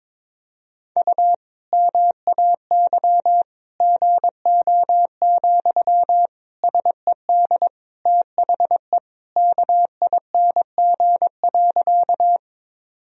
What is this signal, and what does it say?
Morse code, heard as a keyed tone, about 22 wpm, 700 Hz. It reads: U MAY GO, SED THE KING.
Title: Sentences from Top 400 Words and Abbreviations 22wpm